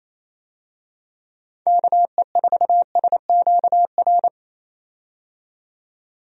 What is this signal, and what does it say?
Morse code, heard as a keyed tone, about 28 wpm, 700 Hz. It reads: KE4SQR
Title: USA Call Signs - 6 Characters - 28wpm